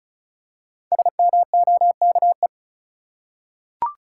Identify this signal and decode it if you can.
Morse code, heard as a keyed tone, about 34 wpm, 700 Hz. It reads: SMOKE
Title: Five Letter Words - 35wpm